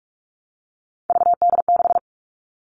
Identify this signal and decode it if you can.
Morse code, heard as a keyed tone, about 44 wpm, 700 Hz. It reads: VD6
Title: All Cut Numbers - 3-Digit 45wpm